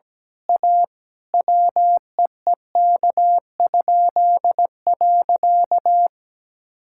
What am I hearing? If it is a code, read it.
Morse code, heard as a keyed tone, about 17 wpm, 700 Hz. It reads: A WEEK?.
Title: Pangram Challenge v2 - 17wpm